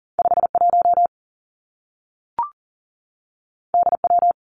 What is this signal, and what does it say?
Morse code, heard as a keyed tone, about 38 wpm, 700 Hz. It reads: 5 1 D W
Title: Two Challenging Letter-Number Combinations 40wpm